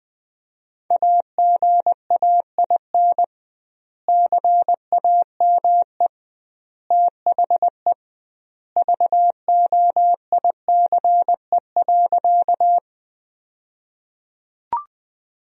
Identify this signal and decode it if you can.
Morse code, heard as a keyed tone, about 20 wpm, 700 Hz. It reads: AGAIN CAME THE VOICE.
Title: 2x Word-Spacing - Sentences from Top 500 Words - 20wpm